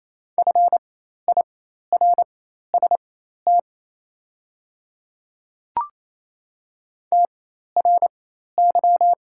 Morse code, heard as a keyed tone, about 28 wpm, spaced as for 13 wpm, 690 Hz. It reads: FIRST TRY
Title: 4x Character Spacing - Top 200 Words - 28wpm